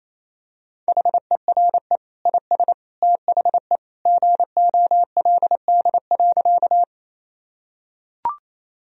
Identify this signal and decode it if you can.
Morse code, heard as a keyed tone, about 28 wpm, 700 Hz. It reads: HERE IS THE GOLD.
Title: Sentences from Top 1000 Words - Review Part 3 - 28wpm